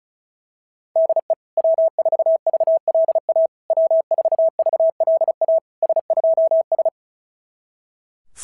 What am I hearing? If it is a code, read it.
Morse code, heard as a keyed tone, about 36 wpm, 660 Hz. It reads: DE W4VLA W4VLA S2S
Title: SOTA QSO Practice 35wpm